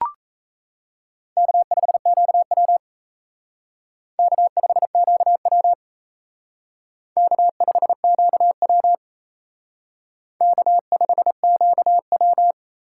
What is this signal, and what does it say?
Morse code, heard as a keyed tone, about 36 wpm, 700 Hz. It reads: K5QW K5QW K5QW K5QW